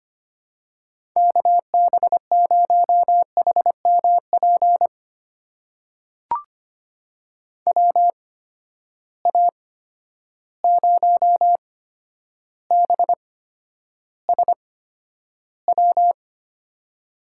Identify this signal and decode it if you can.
Morse code, heard as a keyed tone, about 25 wpm, spaced as for 11 wpm, 700 Hz. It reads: KB0HMP WA0BSW
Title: USA Call Signs - 6 Characters - 8x Character Spacing - 25wpm